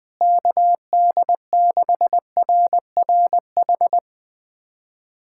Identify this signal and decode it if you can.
Morse code, heard as a keyed tone, about 20 wpm, 700 Hz. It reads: KD6RRH
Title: USA Callsign DE Callsign 20wpm